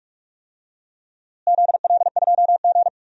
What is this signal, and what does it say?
Morse code, heard as a keyed tone, about 46 wpm, 700 Hz. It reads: ZL2G
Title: DXCC-100 Call Signs - 4 Characters - Repeated 3x - 45wpm